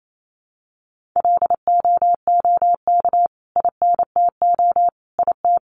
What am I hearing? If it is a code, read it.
Morse code, heard as a keyed tone, about 28 wpm, 700 Hz. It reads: LOOK INTO IT
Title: Sentences from Top 1400 Words 28wpm